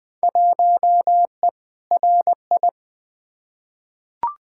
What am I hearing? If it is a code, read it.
Morse code, heard as a keyed tone, about 20 wpm, 700 Hz. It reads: 1E RI
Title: Field Day Exchanges Encore - 20wpm